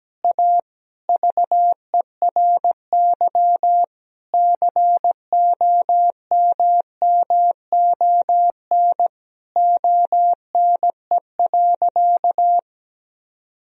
Morse code, heard as a keyed tone, about 17 wpm, 700 Hz. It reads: A VERY COMMON ONE.
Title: Sentences from Top 1000 Words - Review Part 4 - 17wpm